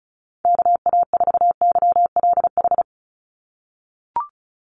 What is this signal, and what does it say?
Morse code, heard as a keyed tone, about 34 wpm, 700 Hz. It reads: KA4YLH
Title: USA Call Signs 35wpm